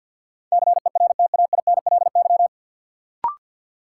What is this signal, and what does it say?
Morse code, heard as a keyed tone, about 49 wpm, 700 Hz. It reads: CERTAINLY
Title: Sentences from Top 900 Words - New Word Drill - 50wpm